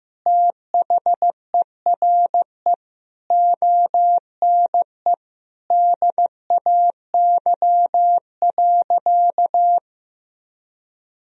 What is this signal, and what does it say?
Morse code, heard as a keyed tone, about 15 wpm, 700 Hz. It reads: THERE ONE DAY.
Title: Sentences from Top 300 Words and Abbreviations 15wpm